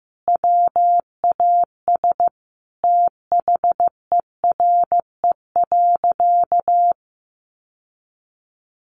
Morse code, heard as a keyed tone, about 15 wpm, 700 Hz. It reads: WAS THERE.
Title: Sentences from Top 700 Words 15wpm